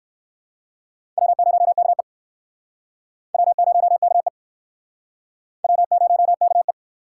Morse code, heard as a keyed tone, about 60 wpm, 700 Hz. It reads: W0KE W0KE W0KE